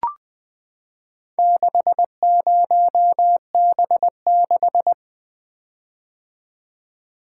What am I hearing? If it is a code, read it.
Morse code, heard as a keyed tone, about 20 wpm, 700 Hz. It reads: 60B6
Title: Four Letter-Number Combinations - 20wpm